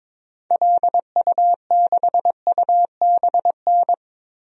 Morse code, heard as a keyed tone, about 21 wpm, 700 Hz. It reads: LU6UBN
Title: DXCC-100 Call Signs - 6 Characters - 22wpm